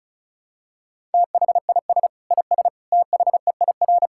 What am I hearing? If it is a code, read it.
Morse code, heard as a keyed tone, about 36 wpm, 700 Hz. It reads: THIS IS THEIR
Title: Sentences from Top 1000 Words - Review Part 5 - 35wpm